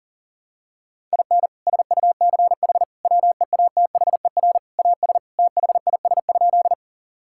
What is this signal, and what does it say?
Morse code, heard as a keyed tone, about 40 wpm, 700 Hz. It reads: IN SUCH WEATHER AS THIS?
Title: Sentences from Top 1000 Words - Review Part 5 - 40wpm